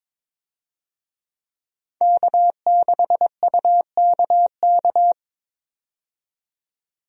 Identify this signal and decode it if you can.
Morse code, heard as a keyed tone, about 22 wpm, 700 Hz. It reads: K6UKK
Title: USA Call Signs - 5 Characters - 22wpm